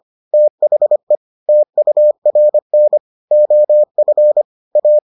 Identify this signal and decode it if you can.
Morse code, heard as a keyed tone, about 25 wpm, 600 Hz. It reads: THE TURN OF A